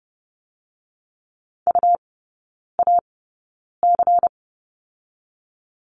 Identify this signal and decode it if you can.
Morse code, heard as a keyed tone, about 30 wpm, spaced as for 9 wpm, 690 Hz. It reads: UAC
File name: Lesson 17e - Letter u - Three Characters 30wpm